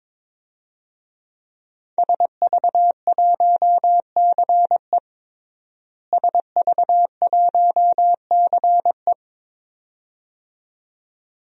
Morse code, heard as a keyed tone, about 22 wpm, 700 Hz. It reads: SV1CE SV1CE
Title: DXCC-100 Call Signs - 5 Characters - Repeated 2x - 22wpm